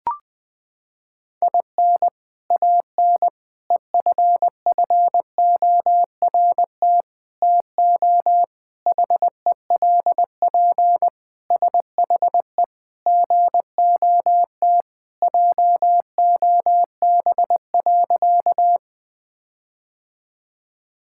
Morse code, heard as a keyed tone, about 20 wpm, 700 Hz. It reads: IN AN EFFORT TO HELP SHE GOT JOB.
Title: Sentences with 4-Word Phrases - 20wpm